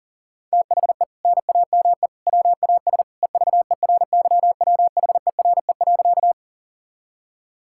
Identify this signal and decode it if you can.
Morse code, heard as a keyed tone, about 40 wpm, 700 Hz. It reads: THE NAME WAS EVERYWHERE.
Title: Sentences from Top 1600 Words 40wpm